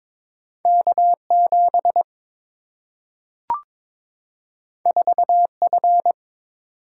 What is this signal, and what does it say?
Morse code, heard as a keyed tone, about 22 wpm, 700 Hz. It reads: K 7 4 F
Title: Single-Letter Single-Number 22wpm